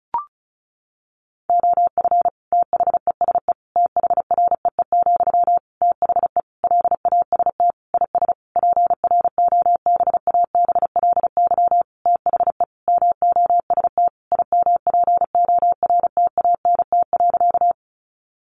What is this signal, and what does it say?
Morse code, heard as a keyed tone, about 34 wpm, 700 Hz. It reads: OF THESE THREE, THE LAST IS PROBABLY THE MOST IMPORTANT.
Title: Sentences from Top 700 Words 35wpm